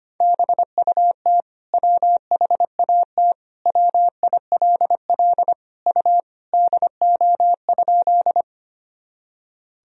Morse code, heard as a keyed tone, about 25 wpm, 700 Hz. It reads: BUT WHAT WILL U DO?
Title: Sentences from Top 100 Words and Abbreviations 25wpm